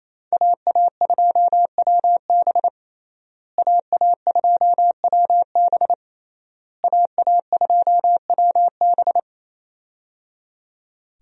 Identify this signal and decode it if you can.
Morse code, heard as a keyed tone, about 28 wpm, 700 Hz. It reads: AA2WB AA2WB AA2WB
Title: USA Call Signs - 5 Characters - Repeated 3x - 28wpm